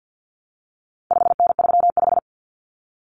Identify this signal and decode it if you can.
Morse code, heard as a keyed tone, about 50 wpm, 700 Hz. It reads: 5N35
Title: Common Cut Numbers - 4-Digit 50wpm